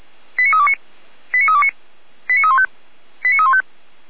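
Five-tone selective call (ZVEI), heard as quick runs of five tones, tones of about 70 ms. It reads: 89329 89329 89315 89315
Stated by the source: ZVEI five-tone selective call 89376 received on an SDR.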